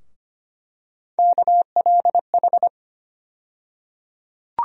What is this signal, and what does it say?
Morse code, heard as a keyed tone, about 25 wpm, 700 Hz. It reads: KLH
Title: USA Call Sign Postfixes - 3 Letters 25wpm